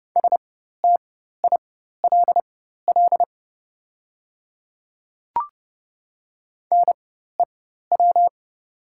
Morse code, heard as a keyed tone, about 31 wpm, spaced as for 14 wpm, 700 Hz. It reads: STILL NEW
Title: 4x Character Spacing - Top 500 Words - 30wpm